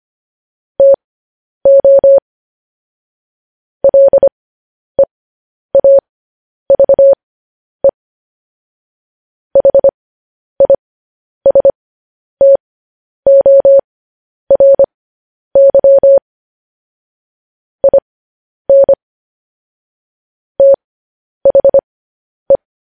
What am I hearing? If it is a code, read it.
Morse code, heard as a keyed tone, about 25 wpm, spaced as for 10 wpm, 550 Hz. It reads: TO LEAVE HISTORY IN THE
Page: news-headlines-in-morse-code-at-10-wpm